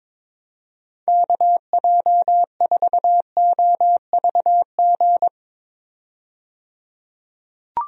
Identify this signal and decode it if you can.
Morse code, heard as a keyed tone, about 22 wpm, 700 Hz. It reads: KJ4OVG
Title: USA Call Signs - 6 Characters - 22wpm